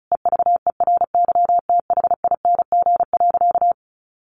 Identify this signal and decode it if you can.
Morse code, heard as a keyed tone, about 34 wpm, 700 Hz. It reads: EVERYTHING.
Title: Sentences from Top 700 Words 35wpm